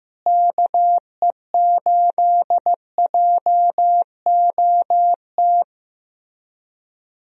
Morse code, heard as a keyed tone, about 15 wpm, 700 Hz. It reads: KE8JOT
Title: USA Call Signs - 6 Characters - Repeated 3x - 15wpm